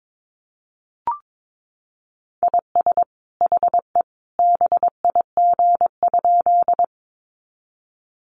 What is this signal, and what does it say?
Morse code, heard as a keyed tone, about 22 wpm, 700 Hz. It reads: IS HE BIG?
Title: Sentences from Top 200 Words 22wpm